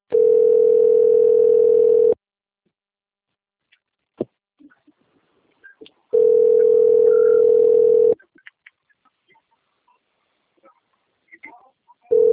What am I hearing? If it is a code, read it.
Ringback tone (North America), 440 + 480 Hz.